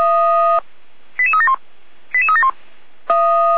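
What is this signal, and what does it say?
Five-tone selective call (ZVEI), heard as quick runs of five tones, tones of about 70 ms. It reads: 88371 88371
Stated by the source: ZVEI five-tone call 88229 received by SDR on channel 455.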